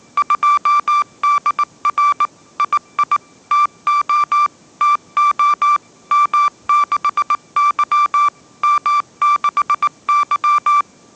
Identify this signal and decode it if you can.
Morse code, heard as a keyed tone, about 19 wpm, 1170 Hz. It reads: 2DR II TO TO M6Y M6Y